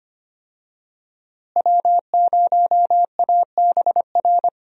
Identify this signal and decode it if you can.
Morse code, heard as a keyed tone, about 25 wpm, 700 Hz. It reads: W0ABR
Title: USA Call Signs - 5 Characters - Repeated 3x - 25wpm